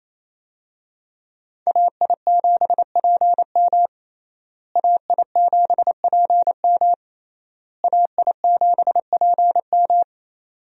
Morse code, heard as a keyed tone, about 28 wpm, 700 Hz. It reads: AI7PM AI7PM AI7PM
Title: USA Call Signs - 5 Characters - Repeated 3x - 28wpm